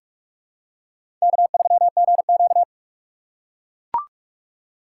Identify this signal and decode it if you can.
Morse code, heard as a keyed tone, about 42 wpm, 700 Hz. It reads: K3GQ
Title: USA Call Signs - 4 Characters - 4x Character Spacing - 45wpm